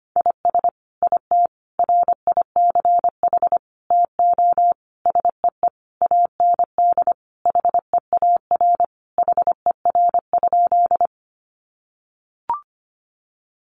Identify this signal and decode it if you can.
Morse code, heard as a keyed tone, about 25 wpm, 700 Hz. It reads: IS IT RICH TO SEE AND HEAR HER?
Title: Sentences from Top 1100 Words 25wpm